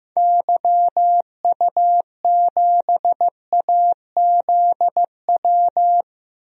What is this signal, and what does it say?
Morse code, heard as a keyed tone, about 15 wpm, 700 Hz. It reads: YU7AZW
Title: DXCC-100 Call Signs - 6 Characters - 15wpm